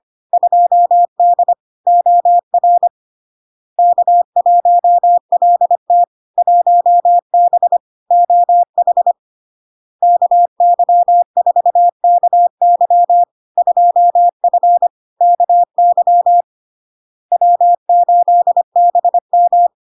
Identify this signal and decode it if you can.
Morse code, heard as a keyed tone, about 25 wpm, 700 Hz. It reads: 2D OR K1LT 1B OH KY4KY 2F KY W8BM
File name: LARK-FD-2022-Log-in-CW-25wpm-700-Hz